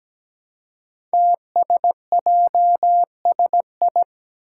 Morse code, heard as a keyed tone, about 17 wpm, 700 Hz. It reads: TSJSI
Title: Five Challenging Letter-Number Combinations 17wpm